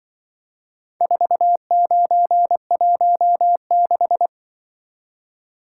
Morse code, heard as a keyed tone, about 23 wpm, 700 Hz. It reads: 4916
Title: Speed-Racing - Four Letter-Number Combinations - 20wpm